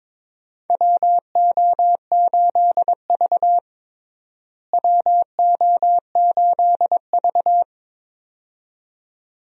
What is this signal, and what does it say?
Morse code, heard as a keyed tone, about 22 wpm, 700 Hz. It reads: WO8V WO8V
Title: USA Call Signs - 4 Characters - Repeated 2x - 22wpm